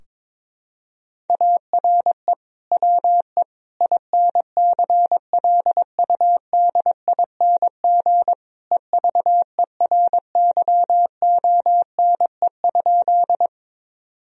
Morse code, heard as a keyed tone, about 22 wpm, 700 Hz. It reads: ARE WE INCLUDING EVERYONE?
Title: Sentences from Top 1000 Words - Review Part 4 - 22wpm